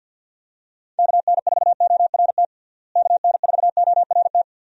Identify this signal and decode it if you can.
Morse code, heard as a keyed tone, about 49 wpm, 700 Hz. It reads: KN4ORT KN4ORT
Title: USA Call Signs - 6 Characters - Repeated 2x - 50wpm